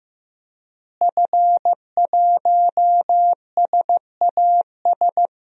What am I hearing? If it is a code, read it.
Morse code, heard as a keyed tone, about 15 wpm, 700 Hz. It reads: F1SAS
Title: DXCC-100 Call Signs - 5 Characters - Repeated 3x - 15wpm